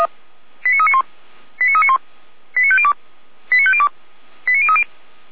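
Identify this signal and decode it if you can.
Five-tone selective call (ZVEI), heard as quick runs of five tones, tones of about 70 ms. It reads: 89391 89391 89502 89502 89030
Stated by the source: ZVEI five-tone selective call received on SDR.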